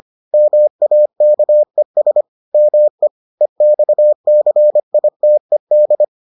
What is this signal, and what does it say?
Morse code, heard as a keyed tone, about 25 wpm, 600 Hz. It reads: MAKES ME EXCITED